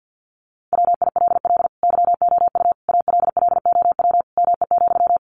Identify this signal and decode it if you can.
Morse code, heard as a keyed tone, about 50 wpm, 700 Hz. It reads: WILL YOU ALLOW ME,
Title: Sentences from Top 700 Words 50wpm